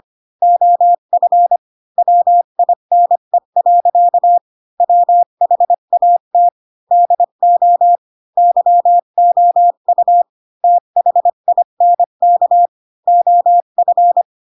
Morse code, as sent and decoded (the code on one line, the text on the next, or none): --- ..-. / .-- .. -. .-.-.- / .-- .... .- - / -.. --- / -.-- --- ..- / - .... .. -. -.- / --- ..-.
OF WINE. WHAT DO YOU THINK OF